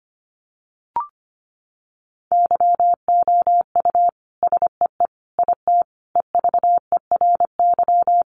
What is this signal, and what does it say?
Morse code, heard as a keyed tone, about 25 wpm, 700 Hz. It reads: YOU SEE IT EVERY
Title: Sentences from Top 200 Words 25wpm